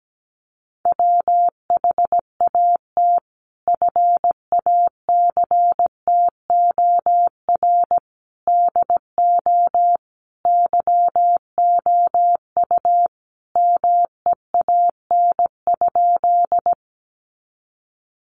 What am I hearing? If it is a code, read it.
Morse code, heard as a keyed tone, about 17 wpm, 700 Hz. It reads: WHAT FACTOR DO YOU MEAN?
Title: Sentences from Top 1300 Words 17wpm